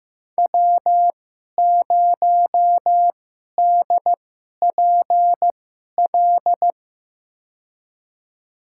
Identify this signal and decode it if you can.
Morse code, heard as a keyed tone, about 15 wpm, spaced as for 11 wpm, 700 Hz. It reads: W0DPL
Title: USA Call Signs - 5 Characters - 2x Character Spacing - 15wpm